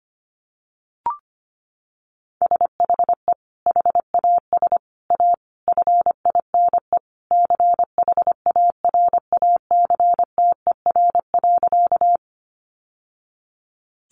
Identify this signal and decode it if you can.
Morse code, heard as a keyed tone, about 25 wpm, 700 Hz. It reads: SHE HAS A FINE CHARACTER.
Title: Sentences from Top 1200 Words 25wpm